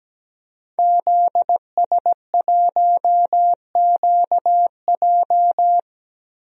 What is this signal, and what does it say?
Morse code, heard as a keyed tone, about 17 wpm, 700 Hz. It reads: ZS1QJ
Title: DXCC-100 Call Signs - 5 Characters - 17wpm